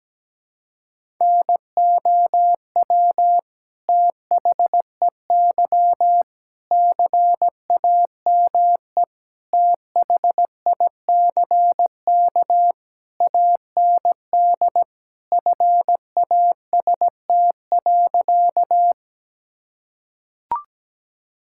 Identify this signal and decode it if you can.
Morse code, heard as a keyed tone, about 17 wpm, 700 Hz. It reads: NOW THEY CAME THICK AND FAST.
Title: Sentences from Top 1600 Words 17wpm